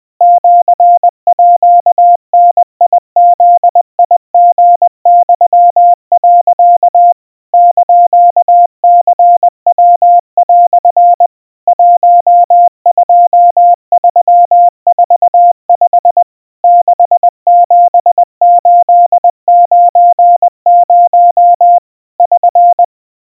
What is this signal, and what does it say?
Morse code (Japanese Wabun code), heard as a keyed tone, about 21 wpm, 700 Hz. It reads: シーダブリユ、 <KN>ニヤ) 12345 67890 <SN>
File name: kawa / Morse sample / aiueo 70 cpm 700 Hz Crispy